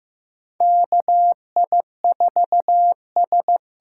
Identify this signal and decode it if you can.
Morse code, heard as a keyed tone, about 15 wpm, 700 Hz. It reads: KI4S
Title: USA Call Signs - 6 Characters - 6x Character Spacing - 15wpm